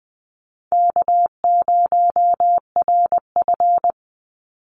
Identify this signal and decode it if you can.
Morse code, heard as a keyed tone, about 20 wpm, 700 Hz. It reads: K0RF
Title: Easy Call Signs Repeated 3x - 20wpm